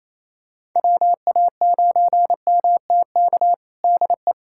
Morse code, heard as a keyed tone, about 28 wpm, 700 Hz. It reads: WA9MTK DE
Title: USA Callsign DE Callsign - 6 Characters - 28wpm